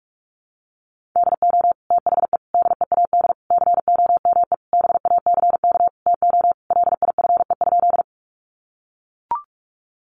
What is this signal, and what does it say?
Morse code, heard as a keyed tone, about 44 wpm, 700 Hz. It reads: DO THE DEAD COME BACK TO LIFE?
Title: Sentences from Top 1500 Words 45wpm